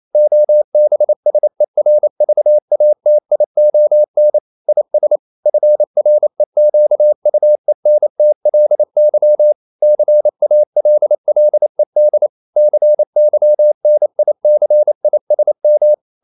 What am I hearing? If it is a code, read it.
Morse code, heard as a keyed tone, about 28 wpm, 600 Hz. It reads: OBSERVATION IS FREQUENTLY CALLED CYNICISM
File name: CW / Practice / EnglishQuotations0128